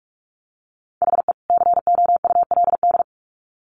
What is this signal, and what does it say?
Morse code, heard as a keyed tone, about 44 wpm, 700 Hz. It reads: HE COULD
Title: Sets of 2 Words 45wpm